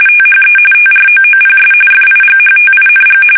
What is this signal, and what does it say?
RTTY, about 45.45 baud. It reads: THE TROOPS FIGHTING
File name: fsk850 / battleshipnj